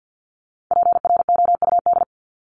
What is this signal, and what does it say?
Morse code, heard as a keyed tone, about 50 wpm, 700 Hz. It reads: PROUD
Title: Top 1501-1750 Words 50wpm